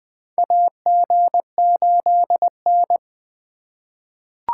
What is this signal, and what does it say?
Morse code, heard as a keyed tone, about 20 wpm, 700 Hz. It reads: AG8N
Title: USA Call Signs - 4x Character Spacing - 20wpm